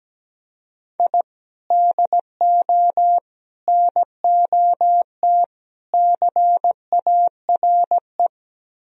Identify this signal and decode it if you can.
Morse code, heard as a keyed tone, about 17 wpm, 700 Hz. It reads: I DO NOT CARE
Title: Speed-Racing - Sets of 4 Words 17wpm